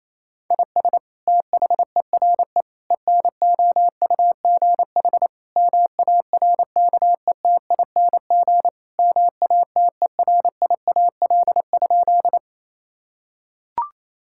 Morse code, heard as a keyed tone, about 28 wpm, 700 Hz. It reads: IS THERE ENOUGH MARKETING MATERIAL?
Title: Sentences from Top 1000 Words - Review Part 5 - 28wpm